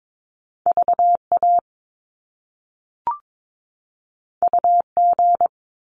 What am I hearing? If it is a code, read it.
Morse code, heard as a keyed tone, about 22 wpm, 700 Hz. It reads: VA UG
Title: Top Two Letter Combinations in Words 22wpm